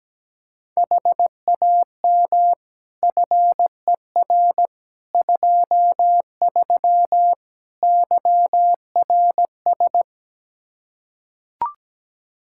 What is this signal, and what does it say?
Morse code, heard as a keyed tone, about 17 wpm, 700 Hz. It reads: HAM FER 23 YRS
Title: Practice QSO - Exchange 3 - More Info - 17wpm